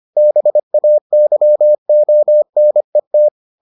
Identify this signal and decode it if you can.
Morse code, heard as a keyed tone, about 25 wpm, 600 Hz. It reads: BAYONET